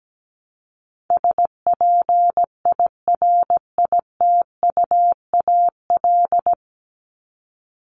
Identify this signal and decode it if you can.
Morse code, heard as a keyed tone, about 17 wpm, 700 Hz. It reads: SPIRITUAL
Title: Top 1501-1750 Words 17wpm